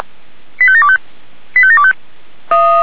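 Five-tone selective call (ZVEI), heard as quick runs of five tones, tones of about 70 ms. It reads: 85725 85725
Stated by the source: Five-tone ZVEI selective call received on an SDR.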